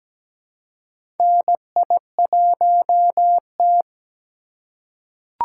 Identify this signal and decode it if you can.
Morse code, heard as a keyed tone, about 17 wpm, 700 Hz. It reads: NI1T
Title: USA Call Signs - 4 Characters - 3x Character Spacing - 17wpm